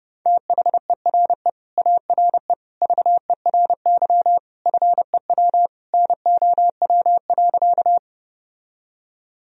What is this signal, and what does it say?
Morse code, heard as a keyed tone, about 30 wpm, 700 Hz. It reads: THERE ARE VERY FEW NOW.
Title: Sentences from Top 1000 Words - Review Part 1 - 30wpm